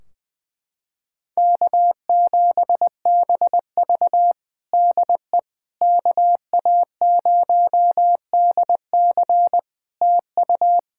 Morse code, heard as a keyed tone, about 20 wpm, 700 Hz. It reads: K7BV DE KA0DC TU
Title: Practice QSO - Exchange 4 - Ending - 20wpm